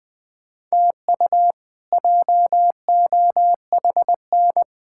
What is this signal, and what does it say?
Morse code, heard as a keyed tone, about 21 wpm, 700 Hz. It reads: TU JOHN
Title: K1USN SST QSO Practice - 20wpm